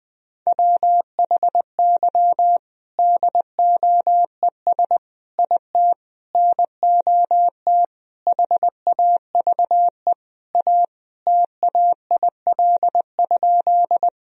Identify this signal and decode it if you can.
Morse code, heard as a keyed tone, about 20 wpm, 700 Hz. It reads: WHY DOES IT NOT HAVE A TAIL?
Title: Sentences from Top 1000 Words - Review Part 3 - 20wpm